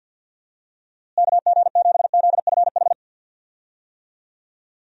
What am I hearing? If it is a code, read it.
Morse code, heard as a keyed tone, about 50 wpm, 700 Hz. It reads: KG7ZFH
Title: USA Call Signs - 6 Characters - Repeated 2x - 50wpm